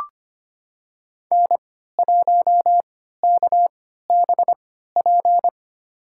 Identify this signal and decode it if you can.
Morse code, heard as a keyed tone, about 25 wpm, spaced as for 14 wpm, 700 Hz. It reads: N1KBP
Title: USA Call Signs - 3x Character Spacing - 25wpm